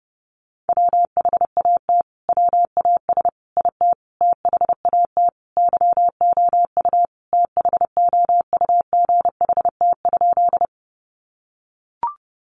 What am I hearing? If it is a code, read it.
Morse code, heard as a keyed tone, about 30 wpm, 700 Hz. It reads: WHAT WAS IT THAT YOU THOUGHT?
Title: Sentences from Top 300 Words 30wpm